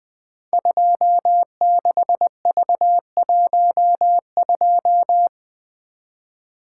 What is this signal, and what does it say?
Morse code, heard as a keyed tone, about 20 wpm, 700 Hz. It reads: 26V12
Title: Code Groups - 5 Letter:Numbers - 20wpm